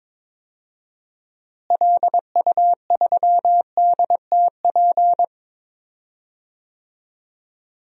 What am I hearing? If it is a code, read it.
Morse code, heard as a keyed tone, about 22 wpm, 700 Hz. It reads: LU3DTP
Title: DXCC-100 Call Signs - 6 Characters - Repeated 3x - 22wpm